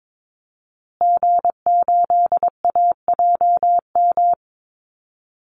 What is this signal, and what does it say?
Morse code, heard as a keyed tone, about 22 wpm, 700 Hz. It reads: G8AJM
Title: Call Signs Repeated 2x - 22wpm